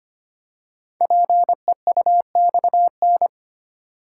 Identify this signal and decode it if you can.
Morse code, heard as a keyed tone, about 25 wpm, 700 Hz. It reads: PEUXN
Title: Code Groups - 5 Letters - 25wpm